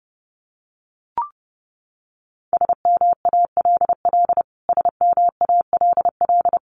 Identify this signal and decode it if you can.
Morse code, heard as a keyed tone, about 30 wpm, 700 Hz. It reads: SMALL SMALL
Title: Top 500 Words Repeated 30wpm